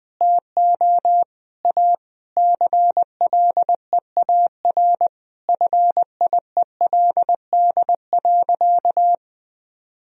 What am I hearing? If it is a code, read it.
Morse code, heard as a keyed tone, about 20 wpm, 700 Hz. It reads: TO A CLEAR FIELD.
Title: Sentences from Top 1000 Words - Review Part 2 - 20wpm